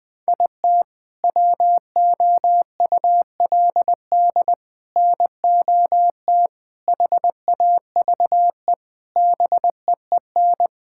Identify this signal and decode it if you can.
Morse code, heard as a keyed tone, about 21 wpm, 700 Hz. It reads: IT WOULD NOT HAVE BEEN
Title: Sentences with 7-Word Phrases - Spoken Beginning - 20wpm